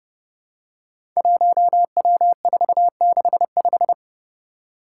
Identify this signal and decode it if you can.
Morse code, heard as a keyed tone, about 30 wpm, 700 Hz. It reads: 1W465
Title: Five Challenging Letter-Number Combinations 30wpm